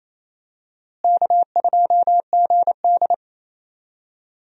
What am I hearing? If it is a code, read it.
Morse code, heard as a keyed tone, about 28 wpm, 700 Hz. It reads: K2GD
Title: USA Call Signs - 4 Characters - Repeated 2x - 28wpm